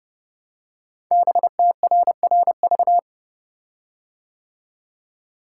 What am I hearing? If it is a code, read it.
Morse code, heard as a keyed tone, about 30 wpm, 700 Hz. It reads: BTRRV
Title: Code Groups - 5 Letters - 30wpm